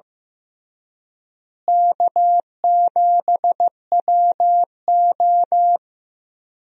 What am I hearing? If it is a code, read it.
Morse code, heard as a keyed tone, about 16 wpm, 700 Hz. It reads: K7WO
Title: USA Call Signs - 4 Characters - Repeated 2x - 15wpm